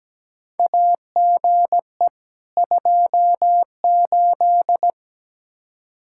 Morse code, heard as a keyed tone, about 17 wpm, 700 Hz. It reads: AGE 28
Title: K1USN SST QSO Practice 17wpm